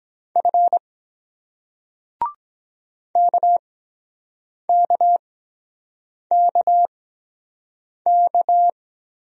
Morse code, heard as a keyed tone, about 23 wpm, spaced as for 7 wpm, 700 Hz. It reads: F KKKK